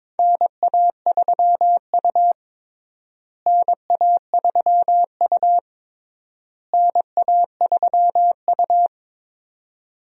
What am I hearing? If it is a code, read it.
Morse code, heard as a keyed tone, about 22 wpm, 700 Hz. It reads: NA3U NA3U NA3U
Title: USA Call Signs - 4 Characters - Repeated 3x - 22wpm